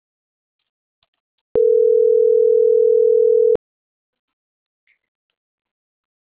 Ringback tone (North America), 440 + 480 Hz.